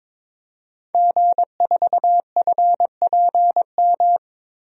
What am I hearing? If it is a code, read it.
Morse code, heard as a keyed tone, about 22 wpm, 700 Hz. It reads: G4FPM
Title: DXCC-100 Call Signs - 5 Characters - Repeated 2x - 22wpm